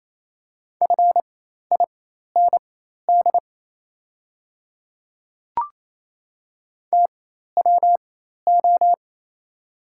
Morse code, heard as a keyed tone, about 28 wpm, spaced as for 13 wpm, 700 Hz. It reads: FIND TWO
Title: 4x Character Spacing - Top 300 Words - 28wpm